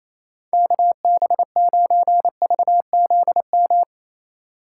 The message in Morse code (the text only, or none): KB9VZM